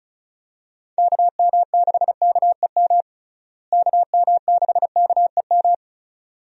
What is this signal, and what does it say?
Morse code, heard as a keyed tone, about 34 wpm, 700 Hz. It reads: KM6KEM KM6KEM
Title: USA Call Signs - 6 Characters - Repeated 2x - 35wpm